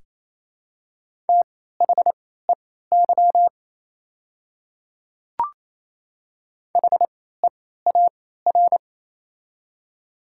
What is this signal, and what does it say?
Morse code, heard as a keyed tone, about 28 wpm, spaced as for 16 wpm, 700 Hz. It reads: THEY HEAR
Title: 3x Character Spacing - Top 300 Words - 28wpm